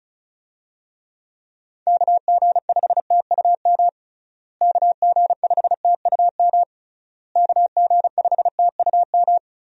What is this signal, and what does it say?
Morse code, heard as a keyed tone, about 36 wpm, 700 Hz. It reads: KG5TUM KG5TUM KG5TUM
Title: USA Call Signs - 6 Characters - Repeated 3x - 35wpm